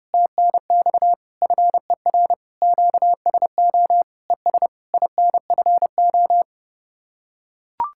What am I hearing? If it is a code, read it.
Morse code, heard as a keyed tone, about 29 wpm, 700 Hz. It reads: TNX FER QSO ES INFO